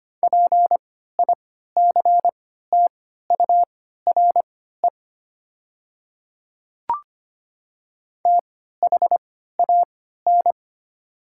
Morse code, heard as a keyed tone, about 25 wpm, spaced as for 14 wpm, 700 Hz. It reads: PICTURE THAN
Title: 3x Character Spacing - Top 300 Words - 25wpm